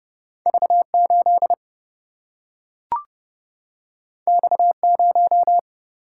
Morse code, heard as a keyed tone, about 30 wpm, 700 Hz. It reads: V 8 X 0
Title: Single-Letter Single-Number 30wpm